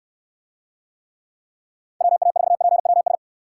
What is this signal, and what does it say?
Morse code, heard as a keyed tone, about 69 wpm, 700 Hz. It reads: WA4PWS